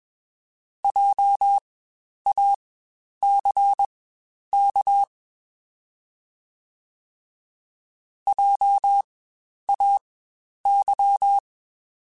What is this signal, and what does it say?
Morse code, heard as a keyed tone, about 21 wpm, spaced as for 10 wpm, 800 Hz. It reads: JACK JAY
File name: Spaced Alphas / Spaced Alphas mp3s / Js sp @20